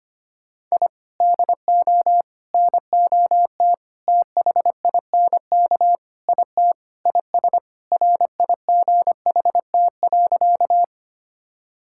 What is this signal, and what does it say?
Morse code, heard as a keyed tone, about 25 wpm, 700 Hz. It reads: I DO NOT THINK IT IS RIGHT.
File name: practice / 25wpm / Sentences with 4-Word Phrases - 25wpm